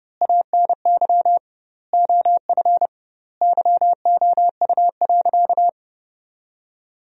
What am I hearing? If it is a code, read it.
Morse code, heard as a keyed tone, about 30 wpm, 700 Hz. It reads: ANY OF YOU.
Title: 2x Word-Spacing - Sentences from Top 400 Words - 30wpm